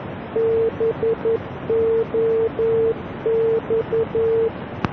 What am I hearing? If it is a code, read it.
Morse code, heard as a keyed tone, about 11 wpm, 450 Hz, with noise. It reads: BOX